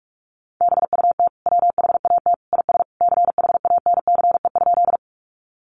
Morse code, heard as a keyed tone, about 44 wpm, 700 Hz. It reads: BUT WHAT IS CHANCE?
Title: Sentences from Top 800 Words 45wpm